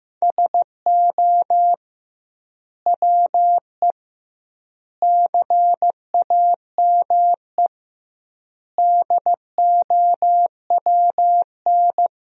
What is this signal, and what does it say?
Morse code, heard as a keyed tone, about 15 wpm, 700 Hz. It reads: SO WE CAME DOWN
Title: 2x Word-Spacing - Sentences from Top 400 Words - 15wpm